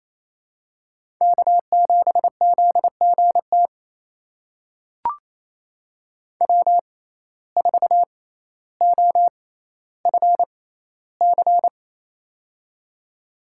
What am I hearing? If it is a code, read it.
Morse code, heard as a keyed tone, about 28 wpm, spaced as for 14 wpm, 700 Hz. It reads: K7ZGT W4OFC
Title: USA Call Signs - 5 Characters - 6x Character Spacing - 28wpm